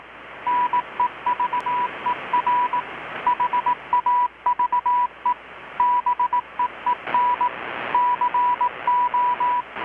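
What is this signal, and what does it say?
Morse code, heard as a keyed tone, about 18 wpm, 980 Hz, with radio noise. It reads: NEVER HAVE BEEN CO
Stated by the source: Morse from KPH's Night of Nights XX maritime radiotelegraphy event, received in Chiapas, Mexico about 3,800 km away on a Kenwood R-600 with a grounded 10 m long-wire antenna.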